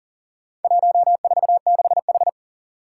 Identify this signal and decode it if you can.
Morse code, heard as a keyed tone, about 40 wpm, 700 Hz. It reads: 146H
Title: Four Challenging Letter-Number Combinations 40wpm